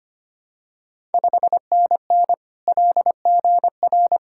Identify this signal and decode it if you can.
Morse code, heard as a keyed tone, about 25 wpm, 700 Hz. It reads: 5NN LGR
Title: Ontario QSO Party Exchange 25wpm